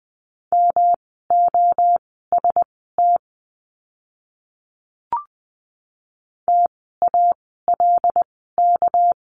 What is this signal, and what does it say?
Morse code, heard as a keyed tone, about 20 wpm, spaced as for 15 wpm, 700 Hz. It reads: MOST TALK
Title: 237 Lesson 38f - Letter q - Words 20wpm